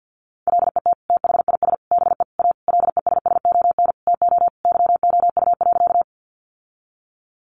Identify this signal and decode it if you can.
Morse code, heard as a keyed tone, about 50 wpm, 700 Hz. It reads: LET THIS BE A LESSON TO YOU.
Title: Sentences from Top 1200 Words 50wpm